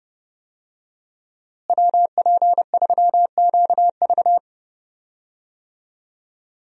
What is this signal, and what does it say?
Morse code, heard as a keyed tone, about 30 wpm, 700 Hz. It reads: WP3QV
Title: USA Call Signs - 5 Characters - Repeated 3x - 30wpm